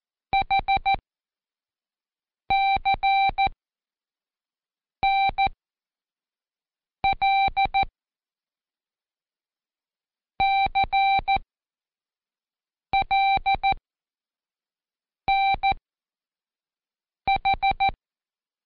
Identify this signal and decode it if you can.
Morse code, heard as a keyed tone, about 14 wpm, spaced as for 5 wpm, 780 Hz. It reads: HCNL CLNH
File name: Code Quick / AudioDownld1 / Seg37-Day4-Quiz CHLN